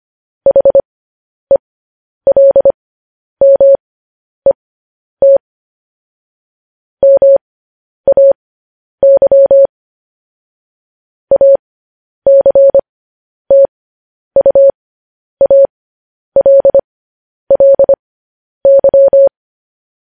Morse code, heard as a keyed tone, about 25 wpm, spaced as for 10 wpm, 550 Hz. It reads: HELMET MAY ACTUALLY
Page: news-headlines-in-morse-code-at-10-wpm